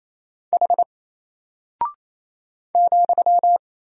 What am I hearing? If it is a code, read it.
Morse code, heard as a keyed tone, about 27 wpm, 700 Hz. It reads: H ,
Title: Single Character (Letter:Number:Punctuation:Prosign) 28wpm